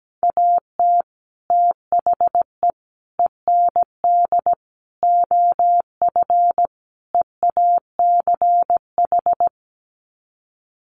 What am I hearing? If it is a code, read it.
Morse code, heard as a keyed tone, about 17 wpm, 700 Hz. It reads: AT THE END OF EACH
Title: Sets of 5 Words Encore 17wpm